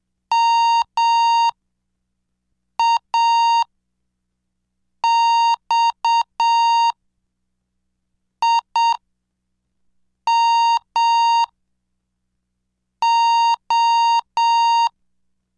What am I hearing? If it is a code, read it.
Morse code, heard as a keyed tone, about 7 wpm, spaced as for 4 wpm, 930 Hz. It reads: MAXIMO
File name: cursos / CDestudo / 17 Track 17